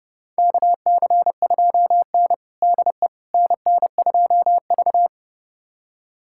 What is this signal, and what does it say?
Morse code, heard as a keyed tone, about 32 wpm, 700 Hz. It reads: KC2N DE NN2V
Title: USA Callsign DE Callsign - 4 Characters - 30wpm